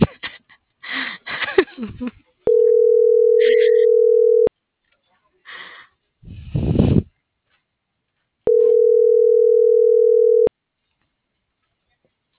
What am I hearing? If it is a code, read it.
Ringback tone (North America), 440 + 480 Hz.